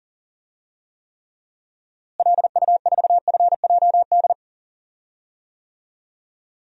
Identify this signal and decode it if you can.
Morse code, heard as a keyed tone, about 40 wpm, 700 Hz. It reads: LU4FJD
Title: DXCC-100 Call Signs - 6 Characters - Repeated 3x - 40wpm